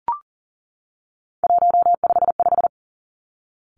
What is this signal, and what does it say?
Morse code, heard as a keyed tone, about 39 wpm, 690 Hz. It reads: 155
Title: Three Challenging Letter-Number Combinations 40wpm